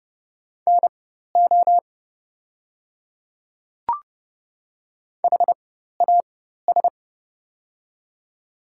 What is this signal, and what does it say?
Morse code, heard as a keyed tone, about 31 wpm, spaced as for 14 wpm, 700 Hz. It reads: NO HAS
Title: 4x Character Spacing - Top 200 Words - 30wpm